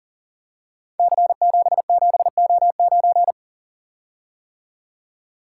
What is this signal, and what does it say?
Morse code, heard as a keyed tone, about 43 wpm, 700 Hz. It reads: C77O9
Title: Code Groups - 5 Letter:Numbers - 40wpm